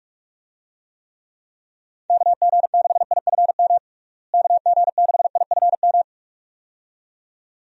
Morse code, heard as a keyed tone, about 46 wpm, 700 Hz. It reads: KG6IFM KG6IFM
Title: USA Call Signs - 6 Characters - Repeated 2x - 45wpm